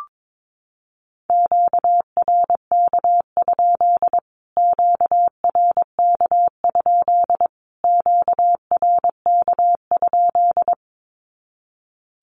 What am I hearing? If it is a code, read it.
Morse code, heard as a keyed tone, about 22 wpm, 700 Hz. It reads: QRK? QRK? QRK?
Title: Q-Codes Repeated 3x 22wpm